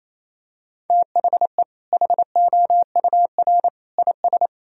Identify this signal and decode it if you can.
Morse code, heard as a keyed tone, about 28 wpm, 700 Hz. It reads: THE HOUR IS
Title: Sentences from Top 1000 Words - Review Part 1 - 28wpm